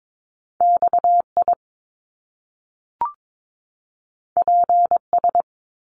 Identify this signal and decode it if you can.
Morse code, heard as a keyed tone, about 22 wpm, 700 Hz. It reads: XI PS
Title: Top Two Letter Combinations in Words 22wpm